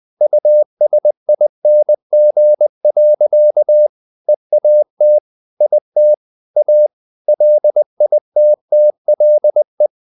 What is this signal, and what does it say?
Morse code, heard as a keyed tone, about 20 wpm, 600 Hz. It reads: USING. EAT IT A LITTLE
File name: the old man and the sea / the old man and the sea part 1 20wpm